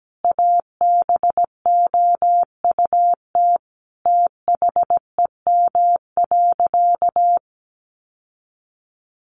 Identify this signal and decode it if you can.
Morse code, heard as a keyed tone, about 17 wpm, 700 Hz. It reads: ABOUT THEM.
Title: Sentences from Top 800 Words 17wpm